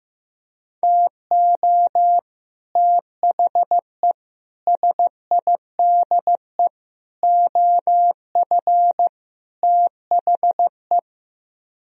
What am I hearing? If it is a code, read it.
Morse code, heard as a keyed tone, about 15 wpm, 700 Hz. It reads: TO THE SIDE OF THE
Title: Speed-Racing - Sets of 5 Words 15wpm